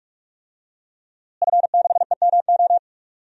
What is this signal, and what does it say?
Morse code, heard as a keyed tone, about 46 wpm, 700 Hz. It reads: F6EMO
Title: DXCC-100 Call Signs - 5 Characters - Repeated 3x - 45wpm